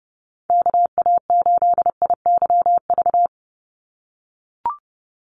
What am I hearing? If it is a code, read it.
Morse code, heard as a keyed tone, about 30 wpm, 700 Hz. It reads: KA8IYV
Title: USA Call Signs 30wpm